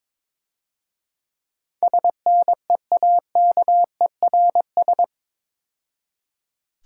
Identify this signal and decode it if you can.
Morse code, heard as a keyed tone, about 27 wpm, 690 Hz. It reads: SNEAKERS
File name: Easy Words - Speed-Racing with Hints - 22wpm